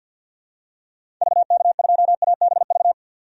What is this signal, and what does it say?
Morse code, heard as a keyed tone, about 50 wpm, 700 Hz. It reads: VK2ABV
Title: DXCC-100 Call Signs - 6 Characters - 50wpm